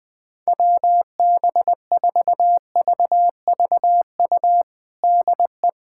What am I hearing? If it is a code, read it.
Morse code, heard as a keyed tone, about 20 wpm, 700 Hz. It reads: WB4VVU DE
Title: USA Callsign DE Callsign - 6 Characters - 20wpm